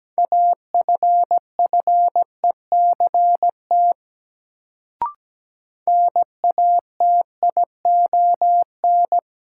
Morse code, heard as a keyed tone, about 17 wpm, 700 Hz. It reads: AFFECT NATION
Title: Sentences from Top 900 Words - New Word Drill - 17wpm